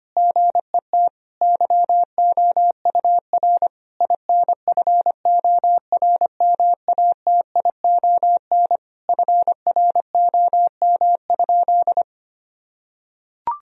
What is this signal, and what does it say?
Morse code, heard as a keyed tone, about 25 wpm, 700 Hz. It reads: GET YOUR INFORMATION FROM?
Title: Sentences from Top 1000 Words - Review Part 3 - 25wpm